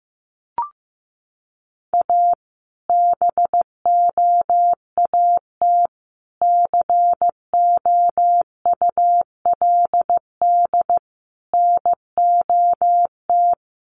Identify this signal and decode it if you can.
Morse code, heard as a keyed tone, about 14 wpm, 700 Hz. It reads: A BOAT COULD NOT
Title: Sentences from Top 1400 Words 15wpm